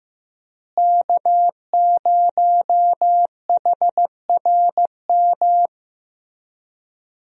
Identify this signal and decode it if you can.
Morse code, heard as a keyed tone, about 15 wpm, 700 Hz. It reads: K0HRM
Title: USA Call Signs - 5 Characters - Repeated 2x - 15wpm